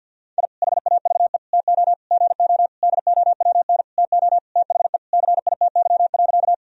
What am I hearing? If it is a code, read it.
Morse code, heard as a keyed tone, about 52 wpm, 700 Hz. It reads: I HAVE TO GO DOWN TO THE CITY.
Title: Sentences from Top 1000 Words - Review Part 1 - 50wpm